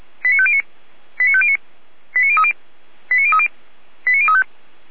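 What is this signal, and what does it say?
Five-tone selective call (ZVEI), heard as quick runs of five tones, tones of about 70 ms. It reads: 89509 89509 89030 89030 89035
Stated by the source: ZVEI five-tone call received by SDR.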